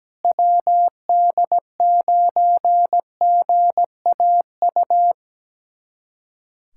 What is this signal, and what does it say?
Morse code, heard as a keyed tone, about 17 wpm, 700 Hz. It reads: WD9GAU
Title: USA Callsign DE Callsign 17wpm